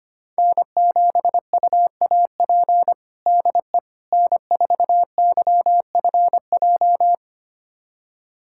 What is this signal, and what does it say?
Morse code, heard as a keyed tone, about 25 wpm, 700 Hz. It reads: N7UAP DE N4YFJ
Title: USA Callsign DE Callsign 25wpm